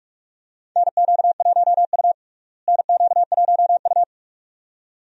Morse code, heard as a keyed tone, about 46 wpm, 700 Hz. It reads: NQ1U NQ1U
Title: USA Call Signs - 4 Characters - Repeated 2x - 45wpm